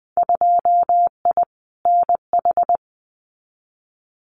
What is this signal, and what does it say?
Morse code, heard as a keyed tone, about 20 wpm, 700 Hz. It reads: 2I NH
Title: Winter Field Day Exchanges Encore 20wpm